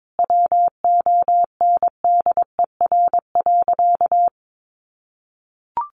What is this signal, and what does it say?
Morse code, heard as a keyed tone, about 22 wpm, 700 Hz. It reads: WONDER.
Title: Sentences from Top 500 Words 22wpm